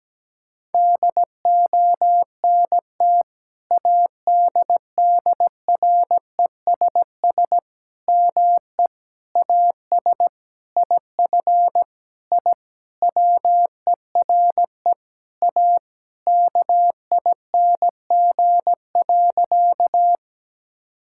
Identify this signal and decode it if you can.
Morse code, heard as a keyed tone, about 17 wpm, 700 Hz. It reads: DONT ADDRESS ME AS IF I WERE A KING.
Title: Sentences from Top 1000 Words - Review Part 6 - 17wpm